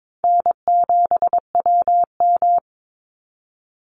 Morse code, heard as a keyed tone, about 22 wpm, 700 Hz. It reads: N7WM
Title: NA SOTA Chaser Callsigns 22wpm